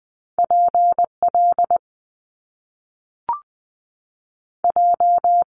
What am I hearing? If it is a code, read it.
Morse code, heard as a keyed tone, about 20 wpm, 700 Hz. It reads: PL J
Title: Top Two Letter Combinations in Words 20wpm